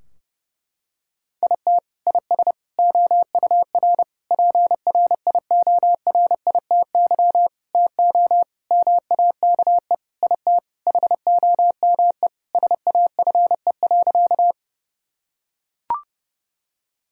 Morse code, heard as a keyed tone, about 32 wpm, 700 Hz. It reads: IT IS OUR PRIORITY TO MAKE IT HOME SAFE.
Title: Sentences from Top 1600 Words 30wpm